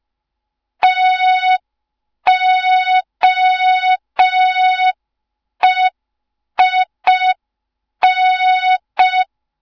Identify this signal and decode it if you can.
Morse code, heard as a keyed tone, about 5 wpm, 3700 Hz. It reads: TOEIN
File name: Morze-HA2MN / hang / 07 GYAKORLAT